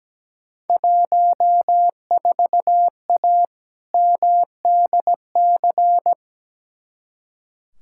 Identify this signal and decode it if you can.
Morse code, heard as a keyed tone, about 17 wpm, 700 Hz. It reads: 14A MDC
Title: Field Day Exchanges Encore - 17wpm